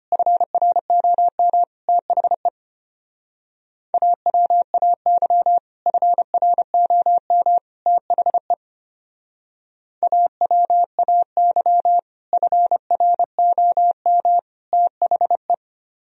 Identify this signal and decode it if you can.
Morse code, heard as a keyed tone, about 31 wpm, 700 Hz. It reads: FROM THE AWAY FROM THE AWAY FROM THE
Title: Speed-Racing - Sets of 3 Words 25wpm